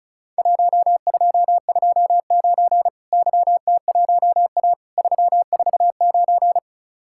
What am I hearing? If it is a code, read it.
Morse code, heard as a keyed tone, about 36 wpm, 700 Hz. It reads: 1229 YT1A 349
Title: Worked All Europe DX Contest - QTC Exchange - 35wpm